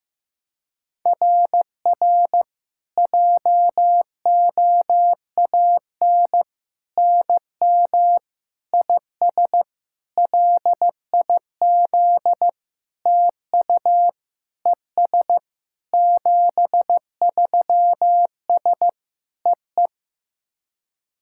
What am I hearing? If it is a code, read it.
Morse code, heard as a keyed tone, about 16 wpm, 700 Hz. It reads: RR JOAN NM IS LIZ TU ES 73S EE
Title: SOTA QSO Practice 15wpm